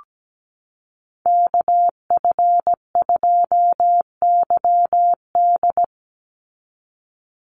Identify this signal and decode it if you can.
Morse code, heard as a keyed tone, about 17 wpm, 700 Hz. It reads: KF2YD
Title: USA Call Signs 17wpm